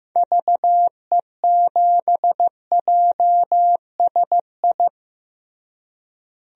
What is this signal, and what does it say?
Morse code, heard as a keyed tone, about 15 wpm, 700 Hz. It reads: VE7JSI
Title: DXCC-100 Call Signs - 6 Characters - Repeated 3x - 15wpm